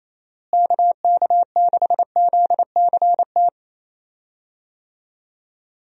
Morse code, heard as a keyed tone, about 28 wpm, 700 Hz. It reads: KK6ZCT
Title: USA Call Signs - 6 Characters - Repeated 3x - 28wpm